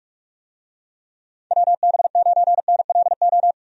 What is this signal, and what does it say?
Morse code, heard as a keyed tone, about 42 wpm, 700 Hz. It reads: WB9NLO
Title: USA Call Signs - 6 Characters - 45wpm